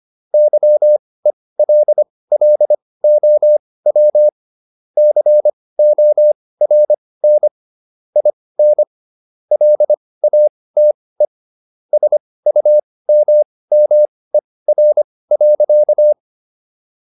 Morse code, heard as a keyed tone, about 25 wpm, spaced as for 18 wpm, 600 Hz. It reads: YELLOW CORN IN LATE SUMMER.